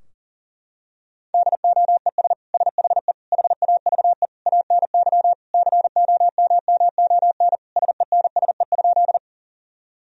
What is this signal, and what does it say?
Morse code, heard as a keyed tone, about 40 wpm, 700 Hz. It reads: DOES SHE HAVE ANY COMMON SENSE?
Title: Sentences from Top 1000 Words - Review Part 4 - 40wpm